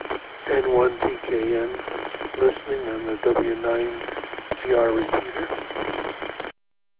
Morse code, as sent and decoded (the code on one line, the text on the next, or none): none